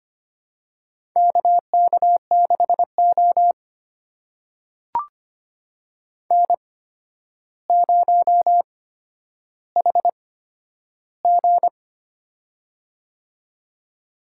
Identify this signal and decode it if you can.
Morse code, heard as a keyed tone, about 25 wpm, spaced as for 11 wpm, 700 Hz. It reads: KK6O N0HG
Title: USA Call Signs - 4 Characters - 8x Character Spacing - 25wpm